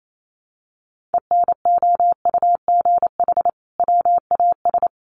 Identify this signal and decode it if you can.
Morse code, heard as a keyed tone, about 28 wpm, 700 Hz. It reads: ENOUGH WAS
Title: Sentences from Top 1400 Words 28wpm